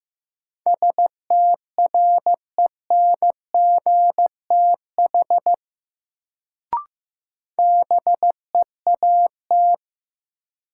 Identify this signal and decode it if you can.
Morse code, heard as a keyed tone, about 15 wpm, 700 Hz. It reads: STRENGTH BEAT
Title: Sentences from Top 1300 Words - New Word Drill Alt - 15wpm